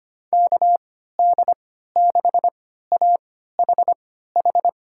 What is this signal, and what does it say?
Morse code, heard as a keyed tone, about 25 wpm, spaced as for 14 wpm, 700 Hz. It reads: KD6AHH
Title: USA Call Signs - 3x Character Spacing - 25wpm